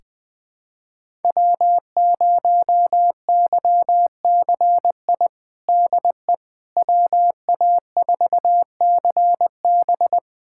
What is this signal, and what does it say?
Morse code, heard as a keyed tone, about 20 wpm, 700 Hz. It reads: W0YCI DE WA4CB
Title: USA Callsign DE Callsign - 5 Characters - 20wpm